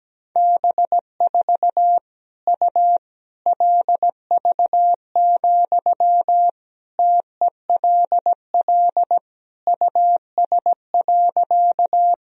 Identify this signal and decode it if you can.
Morse code, heard as a keyed tone, about 17 wpm, 700 Hz. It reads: B4 U LV, TELL US.
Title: Sentences from Top 400 Words and Abbreviations 17wpm